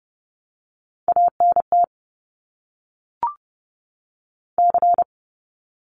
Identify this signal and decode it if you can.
Morse code, heard as a keyed tone, about 30 wpm, 700 Hz. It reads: ANT C
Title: Abbreviations 30wpm